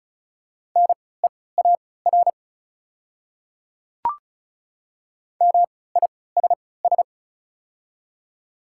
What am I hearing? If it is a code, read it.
Morse code, heard as a keyed tone, about 36 wpm, spaced as for 20 wpm, 700 Hz. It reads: NEAR MISS
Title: 3x Character Spacing - Top 500 Words - 35wpm